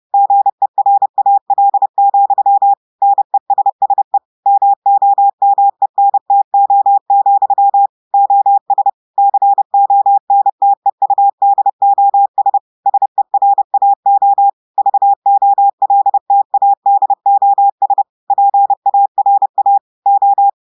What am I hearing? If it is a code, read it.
Morse code, heard as a keyed tone, about 30 wpm, 800 Hz. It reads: GERAL, NESSE MOMENTO, OS CONTEUDOS SERAO VOLTADOS PARA O
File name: audios / portugais / portugais texte en clair 2 vitesse 30 mots minutes